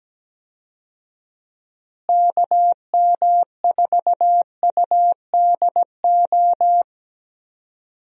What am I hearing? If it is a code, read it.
Morse code, heard as a keyed tone, about 17 wpm, 700 Hz. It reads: KM4UDO
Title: USA Call Signs - 6 Characters - Repeated 2x - 17wpm